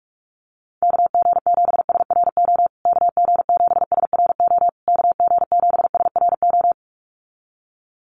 Morse code, heard as a keyed tone, about 44 wpm, 700 Hz. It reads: KG7SRO KG7SRO KG7SRO
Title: USA Call Signs Repeated 3x 45wpm